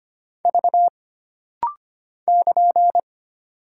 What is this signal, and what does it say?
Morse code, heard as a keyed tone, about 25 wpm, 700 Hz. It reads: V <KN>
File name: Single Character (Letter:Number:Punctuation:Prosign) - Rapid-Fire - 25wpm